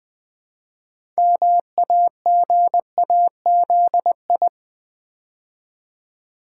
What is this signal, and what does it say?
Morse code, heard as a keyed tone, about 20 wpm, 700 Hz. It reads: MAGAZI
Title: Build A Word - 3 Letters At A Time - 8-Letter Words - 20wpm